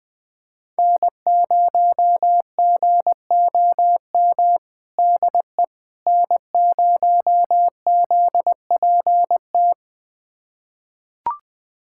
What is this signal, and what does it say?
Morse code, heard as a keyed tone, about 20 wpm, 700 Hz. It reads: N0GOM DE N0ZPT
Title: USA Callsign DE Callsign - 5 Characters - 20wpm